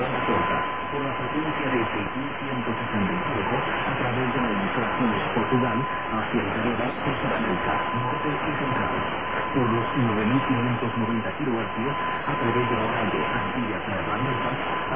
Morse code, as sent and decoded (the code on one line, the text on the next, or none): none